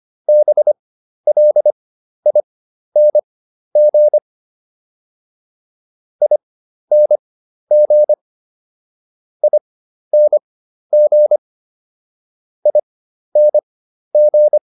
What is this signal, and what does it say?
Morse code, heard as a keyed tone, about 25 wpm, spaced as for 12 wpm, 600 Hz. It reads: BLING ING ING ING